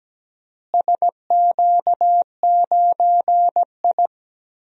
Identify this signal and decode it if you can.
Morse code, heard as a keyed tone, about 17 wpm, 700 Hz. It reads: SQ9I
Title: DXCC-100 Call Signs - 4 Characters - Repeated 3x - 17wpm